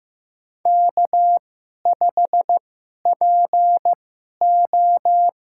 Morse code, heard as a keyed tone, about 15 wpm, spaced as for 11 wpm, 700 Hz. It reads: K5PO
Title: USA Call Signs - 2x Character Spacing - 15wpm